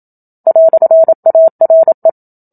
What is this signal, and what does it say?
Morse code, heard as a keyed tone, about 27 wpm, 650 Hz. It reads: "ARE
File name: Book Two / 01-Chapter 1 - 27 WPM